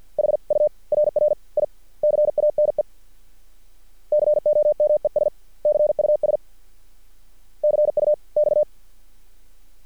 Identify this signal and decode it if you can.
Morse code, heard as a keyed tone, about 50 wpm, spaced as for 38 wpm, 600 Hz. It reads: HURRICANE COMES CUS CUX
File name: archived / CRSnet